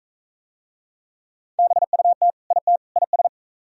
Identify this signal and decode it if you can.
Morse code, heard as a keyed tone, about 38 wpm, 700 Hz. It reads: BUT IT IS